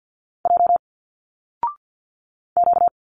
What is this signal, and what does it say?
Morse code, heard as a keyed tone, about 50 wpm, 700 Hz. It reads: J Q
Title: Single Letters 50wpm